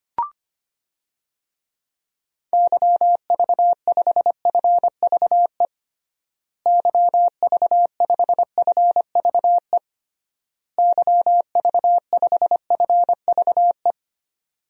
Morse code, heard as a keyed tone, about 25 wpm, 700 Hz. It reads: YV5FVE YV5FVE YV5FVE
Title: DXCC-100 Call Signs - 6 Characters - Repeated 3x - 25wpm